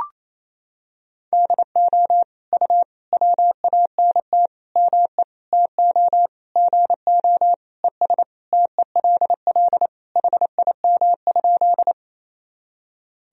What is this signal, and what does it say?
Morse code, heard as a keyed tone, about 27 wpm, 700 Hz. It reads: DO U WANT ME TO GO ES TELL HIM?
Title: Sentences from Top 200 Words and Abbreviations 28wpm